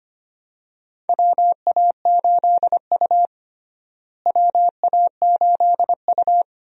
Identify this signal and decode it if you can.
Morse code, heard as a keyed tone, about 25 wpm, 700 Hz. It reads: WA8U WA8U
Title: USA Call Signs - 4 Characters - Repeated 2x - 25wpm